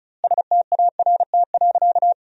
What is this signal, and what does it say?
Morse code, heard as a keyed tone, about 34 wpm, 700 Hz. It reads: START.
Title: 2x Word-Spacing - Sentences from Top 300 Words - 35wpm